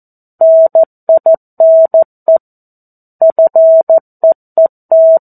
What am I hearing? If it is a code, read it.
Morse code, heard as a keyed tone, about 14 wpm, 650 Hz. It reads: NINE FEET
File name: 02-Chapter-02 14 WPM